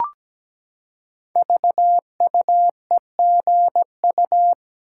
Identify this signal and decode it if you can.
Morse code, heard as a keyed tone, about 17 wpm, 700 Hz. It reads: VUEGU
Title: Code Groups - 5 Letters - 17wpm